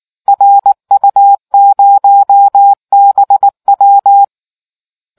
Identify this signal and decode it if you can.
Morse code, heard as a keyed tone, about 19 wpm, 800 Hz. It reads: RU0BW